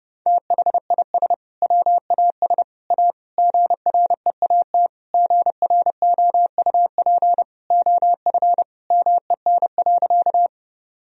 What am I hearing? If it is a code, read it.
Morse code, heard as a keyed tone, about 30 wpm, 700 Hz. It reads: THIS WAS A GREAT GROUP OF MEN.
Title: Sentences from Top 1000 Words - Review Part 1 - 30wpm